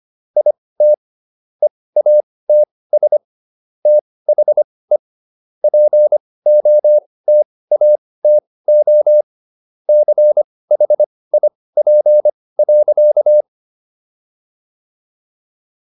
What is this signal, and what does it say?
Morse code, heard as a keyed tone, about 25 wpm, spaced as for 18 wpm, 600 Hz. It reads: IT EATS THE POTATO CHIP.